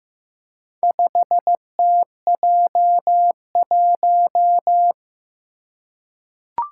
Morse code, heard as a keyed tone, about 15 wpm, 700 Hz. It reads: 5TJ1
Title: Four Challenging Letter-Number Combinations 15wpm